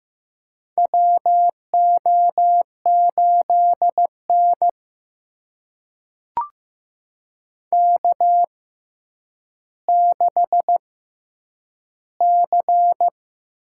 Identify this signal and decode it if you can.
Morse code, heard as a keyed tone, about 15 wpm, spaced as for 9 wpm, 700 Hz. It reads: WO8N K6C
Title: USA Call Signs - 4 Characters - 6x Character Spacing - 15wpm